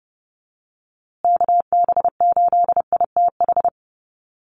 Morse code, heard as a keyed tone, about 30 wpm, 700 Hz. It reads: KB8ITH
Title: USA Call Signs Repeated 2x 30wpm